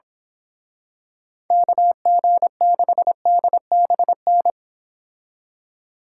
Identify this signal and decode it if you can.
Morse code, heard as a keyed tone, about 26 wpm, 700 Hz. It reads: KG6DBN
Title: Speed Racing - USA Call Signs 22wpm